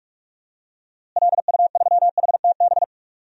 Morse code, heard as a keyed tone, about 46 wpm, 700 Hz. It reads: LU3HTB
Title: DXCC-100 Call Signs - 6 Characters - Repeated 3x - 45wpm